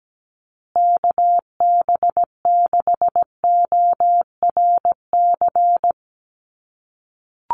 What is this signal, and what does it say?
Morse code, heard as a keyed tone, about 17 wpm, 700 Hz. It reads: KB6ORC
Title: USA Call Signs Repeated 2x 17wpm